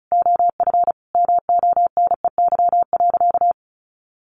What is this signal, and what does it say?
Morse code, heard as a keyed tone, about 34 wpm, 700 Hz. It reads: OF MONEY.
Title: Sentences from Top 1100 Words 35wpm